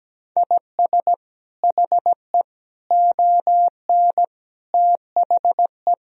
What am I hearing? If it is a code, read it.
Morse code, heard as a keyed tone, about 17 wpm, 700 Hz. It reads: IS HE ON THE
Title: Sentences from Top 1000 Words - Review Part 2 - 17wpm